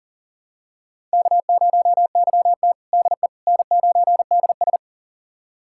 Morse code, heard as a keyed tone, about 40 wpm, 700 Hz. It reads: K0YT DE N9DS
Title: USA Callsign DE Callsign - 4 Characters - 40wpm